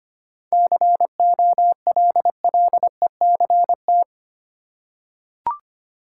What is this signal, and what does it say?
Morse code, heard as a keyed tone, about 25 wpm, 700 Hz. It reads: COLLECT
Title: Seven Letter Words - 25wpm